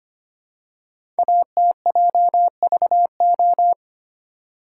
Morse code, heard as a keyed tone, about 25 wpm, 700 Hz. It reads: ATJVO
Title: Code Groups - 5 Letters - 25wpm